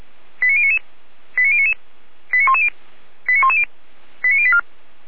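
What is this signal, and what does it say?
Five-tone selective call (ZVEI), heard as quick runs of five tones, tones of about 70 ms. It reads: 89099 89099 89110 89110 89084
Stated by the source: ZVEI five-tone selective call received on SDR.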